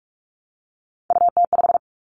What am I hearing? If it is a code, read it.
Morse code, heard as a keyed tone, about 46 wpm, 700 Hz. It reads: UT5
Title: Callsign Prefixes for Top 25 DXCC - 45wpm